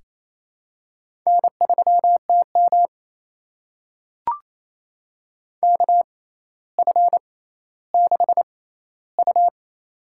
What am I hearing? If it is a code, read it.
Morse code, heard as a keyed tone, about 28 wpm, spaced as for 14 wpm, 700 Hz. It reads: N3TM KF6U
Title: USA Call Signs - 4 Characters - 6x Character Spacing - 28wpm